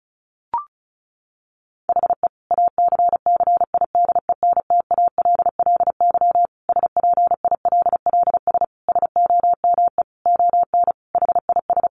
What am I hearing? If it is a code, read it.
Morse code, heard as a keyed tone, about 34 wpm, 700 Hz. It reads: HE ACCIDENTALLY SPILLS SOME ON HIS
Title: Additional Easy Phrases 35wpm